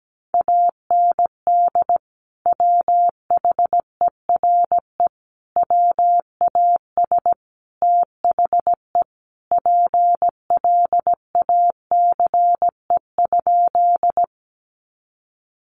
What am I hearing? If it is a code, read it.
Morse code, heard as a keyed tone, about 17 wpm, 700 Hz. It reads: AND WHERE WAS THE PLACE?
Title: Sentences from Top 200 Words 17wpm